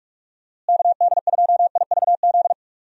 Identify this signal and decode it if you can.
Morse code, heard as a keyed tone, about 46 wpm, 700 Hz. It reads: KD2IVZ
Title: USA Call Signs - 6 Characters - Repeated 2x - 45wpm